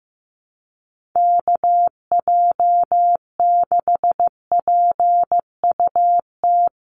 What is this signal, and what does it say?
Morse code, heard as a keyed tone, about 15 wpm, 700 Hz. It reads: KJ6PUT
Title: USA Call Signs 15wpm